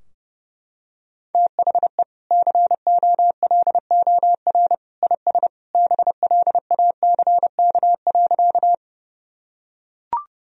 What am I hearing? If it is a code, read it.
Morse code, heard as a keyed tone, about 32 wpm, 700 Hz. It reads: THE COLOR IS BLACK.
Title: Sentences from Top 1000 Words - Review Part 2 - 30wpm